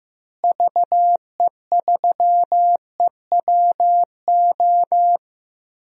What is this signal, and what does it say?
Morse code, heard as a keyed tone, about 15 wpm, 700 Hz. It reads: VE3EWO
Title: DXCC-100 Call Signs - 6 Characters - Repeated 3x - 15wpm